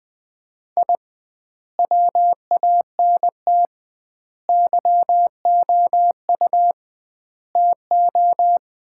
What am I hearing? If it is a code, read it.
Morse code, heard as a keyed tone, about 20 wpm, 700 Hz. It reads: I WANT YOU TO
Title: 2x Word-Spacing - Sets of 4 Words - 20wpm